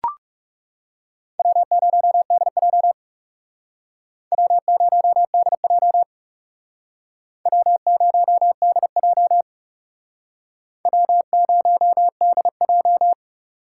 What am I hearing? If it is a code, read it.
Morse code, heard as a keyed tone, about 39 wpm, 700 Hz. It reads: W0DJ W0DJ W0DJ W0DJ